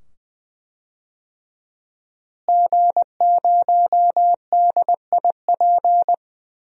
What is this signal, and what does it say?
Morse code, heard as a keyed tone, about 20 wpm, 700 Hz. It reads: G0DIP
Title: DXCC-100 Call Signs - 5 Characters - Repeated 2x - 20wpm